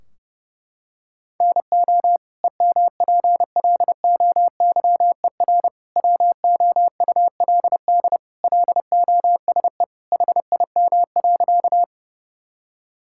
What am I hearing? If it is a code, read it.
Morse code, heard as a keyed tone, about 28 wpm, 700 Hz. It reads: NO EMPLOYER WOULD LOSE HIM.
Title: Sentences from Top 1600 Words 30wpm